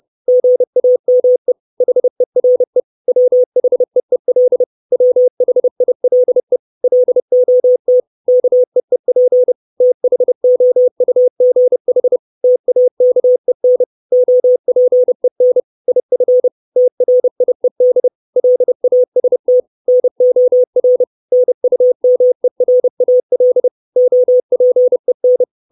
Morse code, heard as a keyed tone, about 30 wpm, 500 Hz. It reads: GAME HERE WHEEL WHILE LOT KEEP THOUGH TAKEN OPEN IF TRIED LAST NOR NUMERAL OPEN
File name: 632 Words 30WPM